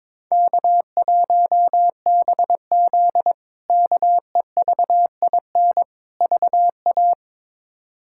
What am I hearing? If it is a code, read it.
Morse code, heard as a keyed tone, about 22 wpm, 700 Hz. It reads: K1BZ KEVIN VA
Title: K1USN SST QSO Practice - 22wpm